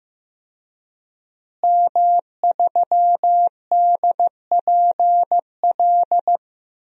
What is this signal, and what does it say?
Morse code, heard as a keyed tone, about 15 wpm, 700 Hz. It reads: M3DPL
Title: DXCC-100 Call Signs - 5 Characters - Repeated 3x - 15wpm